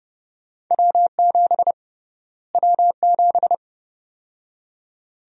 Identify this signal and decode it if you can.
Morse code, heard as a keyed tone, about 30 wpm, 700 Hz. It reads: W7 W7
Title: USA Call Sign Prefixes - Repeated 2x - 30wpm